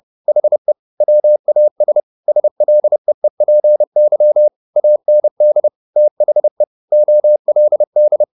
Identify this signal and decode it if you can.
Morse code, heard as a keyed tone, about 30 wpm, 600 Hz. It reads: HE WAS SLEEPY AND THE OLD
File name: the old man and the sea / the old man and the sea 1 30wpm